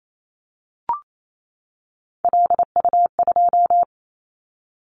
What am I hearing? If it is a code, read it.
Morse code, heard as a keyed tone, about 28 wpm, 700 Hz. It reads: LU2
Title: Callsign Prefixes for Top 100 DXCC - 28wpm